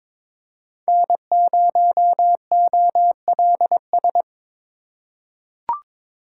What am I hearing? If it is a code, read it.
Morse code, heard as a keyed tone, about 22 wpm, 700 Hz. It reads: N0OLS
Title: USA Call Signs - 5 Characters - 4x Character Spacing - 22wpm